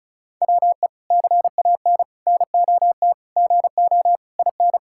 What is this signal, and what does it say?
Morse code, heard as a keyed tone, about 34 wpm, 700 Hz. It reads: WE CAN NOT GO IN
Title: Sentences from Top 1000 Words - Review Part 6 - 35wpm